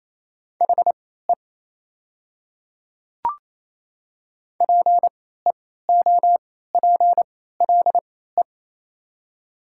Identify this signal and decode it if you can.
Morse code, heard as a keyed tone, about 28 wpm, spaced as for 16 wpm, 700 Hz. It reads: HE PEOPLE
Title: 3x Character Spacing - Top 100 Words - 28wpm